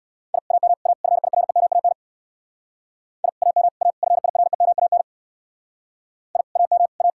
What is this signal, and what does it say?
Morse code, heard as a keyed tone, about 62 wpm, 700 Hz. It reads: I AM A FIREMAN I AM A FIREMAN I AM A
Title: Speed-Racing Common QSO Phrases Encore - 50wpm